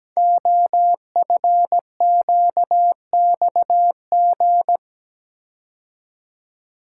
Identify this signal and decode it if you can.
Morse code, heard as a keyed tone, about 17 wpm, 700 Hz. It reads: OFQXG
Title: Code Groups - 5 Letters - 17wpm